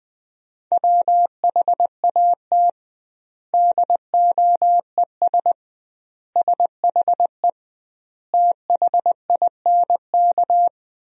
Morse code, heard as a keyed tone, about 20 wpm, 700 Hz. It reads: WHAT DOES SHE THINK
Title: 2x Word-Spacing - Sentences from Top 200 Words - 20wpm